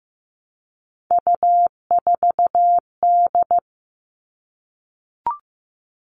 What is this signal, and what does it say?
Morse code, heard as a keyed tone, about 15 wpm, 700 Hz. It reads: U4D
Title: All Cut Numbers - 3-Digit 15wpm